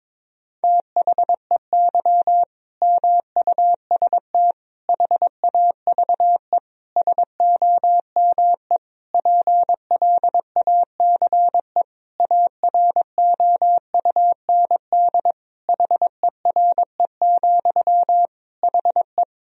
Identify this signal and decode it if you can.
Morse code, heard as a keyed tone, about 22 wpm, 700 Hz. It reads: THEY MUST HAVE SOME PLACE AROUND HERE, HE
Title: Sentences from Top 1000 Words - Review Part 3 - 22wpm